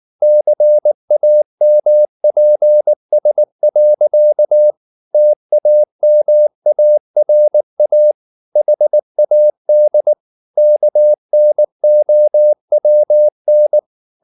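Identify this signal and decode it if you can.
Morse code, heard as a keyed tone, about 19 wpm, 600 Hz. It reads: CAMPS. TAMARA HAD KNOWN
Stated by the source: Morse code at 19 wpm from CW med Gnister, reading Ken Follett's novel Never in English.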